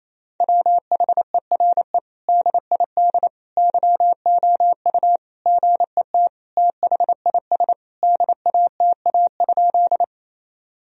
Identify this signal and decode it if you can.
Morse code, heard as a keyed tone, about 27 wpm, 700 Hz. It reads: WHERE DID YOU GET THIS DATA?
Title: Sentences from Top 1000 Words - Review Part 3 - 28wpm